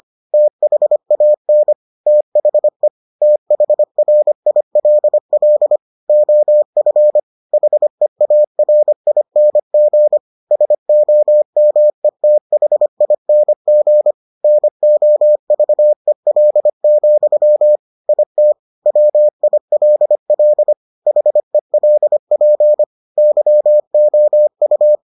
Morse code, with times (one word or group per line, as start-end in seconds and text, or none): 0.3-1.7 THAN
2.1-2.9 THE
3.2-5.8 THRILL
6.1-7.2 OF
7.5-10.2 HEARING
10.5-14.1 SOMETHING
14.4-17.8 NOVEL,
18.1-18.5 IT
18.9-20.7 WILL
21.1-22.8 HELP
23.2-25.0 YOU